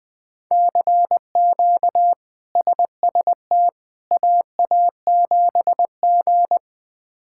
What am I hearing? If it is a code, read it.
Morse code, heard as a keyed tone, about 20 wpm, 700 Hz. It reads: CQ SST AA7G
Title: K1USN SST QSO Practice - 20wpm